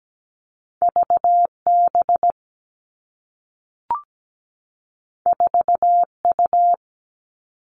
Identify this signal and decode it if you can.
Morse code, heard as a keyed tone, about 17 wpm, 700 Hz. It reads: VB 4U
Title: Two Challenging Letter-Number Combinations 17wpm